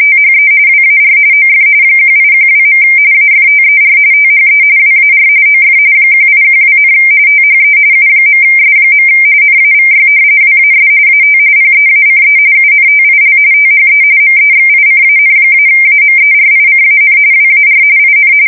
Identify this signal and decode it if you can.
RTTY, about 45.45 baud. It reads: THE WAR OF 1812. AFTER THE WAR OF 1812, THE U.S. NAVY WAS AT PEACE UNTIL THE MEXICAN-AMERICAN WAR IN
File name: fsk170 / usnavy1